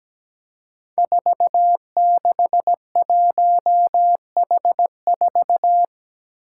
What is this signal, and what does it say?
Morse code, heard as a keyed tone, about 17 wpm, 700 Hz. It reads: 461H4
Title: Five Challenging Letter-Number Combinations 17wpm